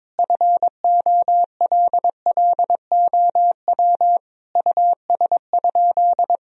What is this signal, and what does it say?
Morse code, heard as a keyed tone, about 22 wpm, 700 Hz. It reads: FOLLOW US?
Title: Sentences from Top 1000 Words - Review Part 1 - 22wpm